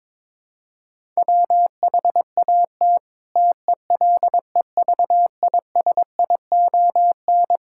Morse code, heard as a keyed tone, about 22 wpm, 700 Hz. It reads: WHAT TELEVISION
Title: Sentences from Top 1000 Words - Review Part 5 - 22wpm